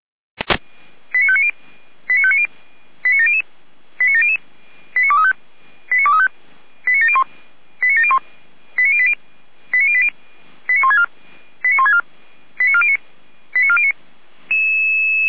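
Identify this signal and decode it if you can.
Five-tone selective call (ZVEI), heard as quick runs of five tones, tones of about 70 ms. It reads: 89590 89590 89700 89700 89235 89235 89701 89701 89080 89080 89164 89164 89409 89409
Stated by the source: ZVEI five-tone selective call 89092, received by SDR.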